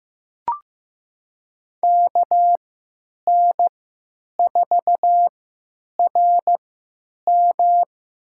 Morse code, heard as a keyed tone, about 15 wpm, spaced as for 9 wpm, 700 Hz. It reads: KN4RM
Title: USA Call Signs - 3x Character Spacing - 15wpm